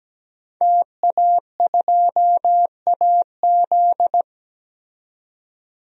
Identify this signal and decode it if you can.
Morse code, heard as a keyed tone, about 17 wpm, 700 Hz. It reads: TA2AZ
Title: DXCC-100 Call Signs - 5 Characters - 17wpm